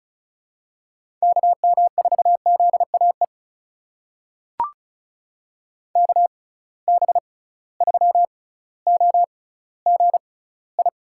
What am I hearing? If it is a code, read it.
Morse code, heard as a keyed tone, about 36 wpm, spaced as for 18 wpm, 700 Hz. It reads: KM4ZAE KB3OGI
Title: USA Call Signs - 6 Characters - 6x Character Spacing - 35wpm